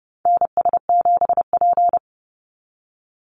Morse code, heard as a keyed tone, about 30 wpm, 700 Hz. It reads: NS7P
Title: NA SOTA Activator Callsigns 30wpm